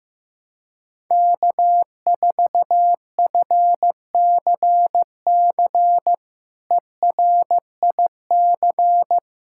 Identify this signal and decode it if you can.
Morse code, heard as a keyed tone, about 15 wpm, 700 Hz. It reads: K4FCC ERIC
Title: K1USN SST QSO Practice - 15wpm